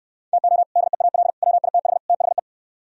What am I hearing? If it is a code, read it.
Morse code, heard as a keyed tone, about 69 wpm, 700 Hz. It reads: TO DEAL WITH THE